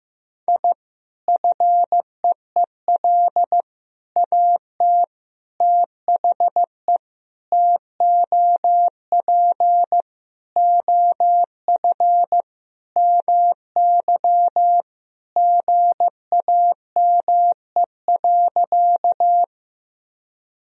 Morse code, heard as a keyed tone, about 15 wpm, 700 Hz. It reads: I FEEL AT THE TOP OF MY GAME.
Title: Sentences from Top 1000 Words - Review Part 2 - 15wpm